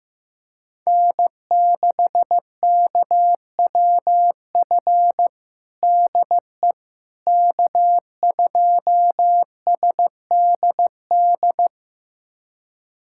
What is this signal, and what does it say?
Morse code, heard as a keyed tone, about 15 wpm, 700 Hz. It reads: N6KWF DE K2SDD
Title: USA Callsign DE Callsign - 5 Characters - 15wpm